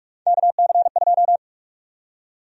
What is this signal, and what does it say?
Morse code, heard as a keyed tone, about 43 wpm, 700 Hz. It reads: KC2
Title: USA Call Sign Prefixes - 45wpm